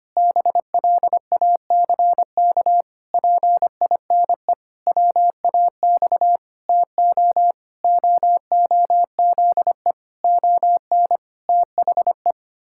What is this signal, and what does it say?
Morse code, heard as a keyed tone, about 25 wpm, 700 Hz. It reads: BLACK PINE WAX TO OOZE ON THE
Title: Pangram Challenge v2 - 25wpm